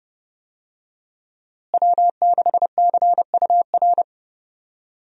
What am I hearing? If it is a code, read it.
Morse code, heard as a keyed tone, about 32 wpm, 700 Hz. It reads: W6CUR
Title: USA Call Signs - 5 Characters - Repeated 2x - 30wpm